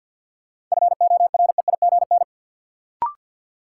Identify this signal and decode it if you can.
Morse code, heard as a keyed tone, about 46 wpm, 700 Hz. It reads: FOREIGN
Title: Sentences from Top 1400 Words - New Word Drill - 50wpm